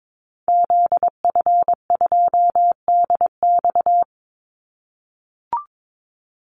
Morse code, heard as a keyed tone, about 22 wpm, 700 Hz. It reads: ZF2DX
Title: Call Signs - 22wpm